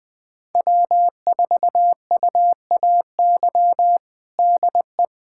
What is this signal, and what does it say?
Morse code, heard as a keyed tone, about 20 wpm, 700 Hz. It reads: W4UAY DE
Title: USA Callsign DE Callsign - 5 Characters - 20wpm